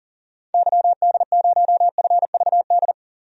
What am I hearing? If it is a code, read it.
Morse code, heard as a keyed tone, about 40 wpm, 700 Hz. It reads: YD0FVD
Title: DXCC-100 Call Signs - 6 Characters - 40wpm